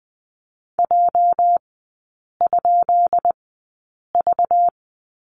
Morse code, heard as a keyed tone, about 20 wpm, spaced as for 8 wpm, 700 Hz. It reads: J?V
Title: 205 Lesson 33e - Letter j - Three Characters 20wpm